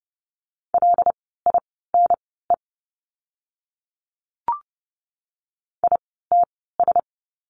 Morse code, heard as a keyed tone, about 30 wpm, spaced as for 17 wpm, 700 Hz. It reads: LINE ITS
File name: course / 30:10wpm / 115 Lesson 19f - Letter w - Words 30wpm